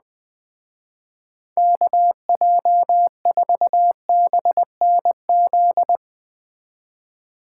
Morse code, heard as a keyed tone, about 20 wpm, 700 Hz. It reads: KJ4BNZ
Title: Speed Racing - USA 2x3 Call Signs 17wpm